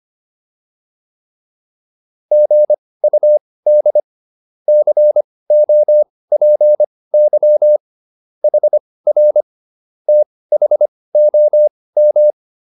Morse code, heard as a keyed tone, about 25 wpm, spaced as for 18 wpm, 600 Hz. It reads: GUD COPY HR THOM